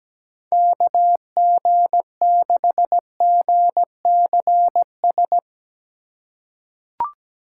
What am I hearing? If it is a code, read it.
Morse code, heard as a keyed tone, about 17 wpm, 700 Hz. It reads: KG6GCS
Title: USA Call Signs - 4x Character Spacing - 17wpm